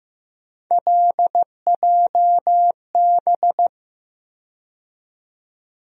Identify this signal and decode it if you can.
Morse code, heard as a keyed tone, about 15 wpm, 700 Hz. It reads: LJB
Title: USA Call Sign Postfixes - 3 Letters 15wpm